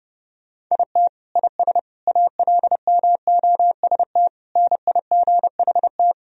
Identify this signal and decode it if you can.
Morse code, heard as a keyed tone, about 30 wpm, 700 Hz. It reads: IT IS ALMOST NIGHT
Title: Sentences from Top 1000 Words - Review Part 4 - 30wpm